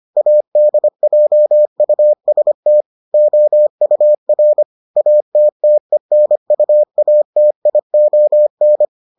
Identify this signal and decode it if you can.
Morse code, heard as a keyed tone, about 25 wpm, 600 Hz. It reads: ADJUST OUR ATTENUATION